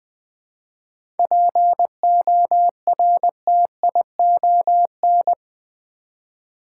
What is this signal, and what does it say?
Morse code, heard as a keyed tone, about 20 wpm, 700 Hz. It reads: PORTION
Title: 7 Letter Words - First 3 Letters Given - 20wpm